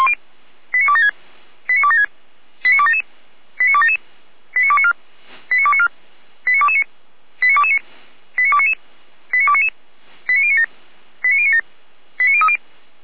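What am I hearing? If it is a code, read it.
Five-tone selective call (ZVEI), heard as quick runs of five tones, tones of about 70 ms. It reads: 89276 89276 89277 89277 89294 89294 89229 89229 89299 89299 89097 89097 89030
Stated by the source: ZVEI five-tone call received on SDR.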